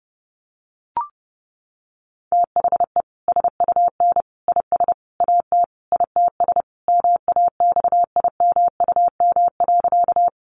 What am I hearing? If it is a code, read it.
Morse code, heard as a keyed tone, about 30 wpm, 700 Hz. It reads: THE SUN IS AT ITS MAXIMUM.
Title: Sentences from Top 1300 Words 30wpm